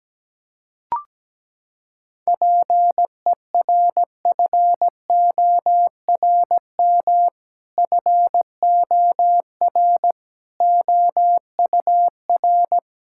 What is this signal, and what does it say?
Morse code, heard as a keyed tone, about 17 wpm, 700 Hz. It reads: PERFORM FOR OUR
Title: Sentences from Top 1000 Words - Review Part 5 - 17wpm